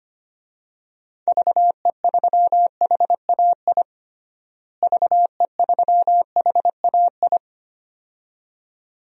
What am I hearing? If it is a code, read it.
Morse code, heard as a keyed tone, about 25 wpm, 700 Hz. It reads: VE3HAI VE3HAI
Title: DXCC-100 Call Signs - 6 Characters - Repeated 2x - 25wpm